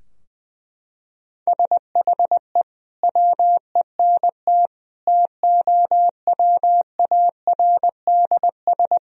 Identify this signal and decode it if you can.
Morse code, heard as a keyed tone, about 20 wpm, 700 Hz. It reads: SHE WENT TOWARDS
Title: Sentences from Top 1000 Words - Review Part 2 - 20wpm